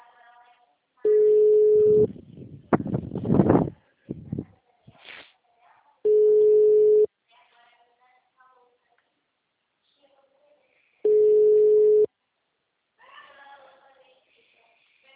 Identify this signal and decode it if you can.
Ringback tone (Europe, 425 Hz), with speech.